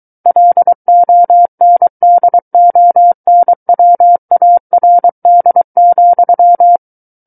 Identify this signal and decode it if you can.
Morse code, heard as a keyed tone, about 23 wpm, 700 Hz. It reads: LONDONWARD,
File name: Book One / 14-Chapter 14 - 23 WPM